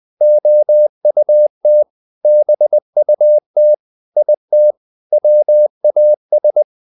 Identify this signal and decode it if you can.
Morse code, heard as a keyed tone, about 20 wpm, 600 Hz. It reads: OUT BUT IT WAS
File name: the old man and the sea part 2 20wpm